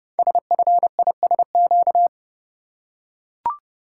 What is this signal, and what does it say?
Morse code, heard as a keyed tone, about 30 wpm, 700 Hz. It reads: SFISQ
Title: Code Groups - 5 Letter:Numbers - 30wpm